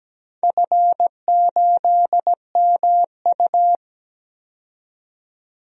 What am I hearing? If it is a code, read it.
Morse code, heard as a keyed tone, about 17 wpm, 700 Hz. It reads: F8MU
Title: DXCC-100 Call Signs - 4 Characters - 17wpm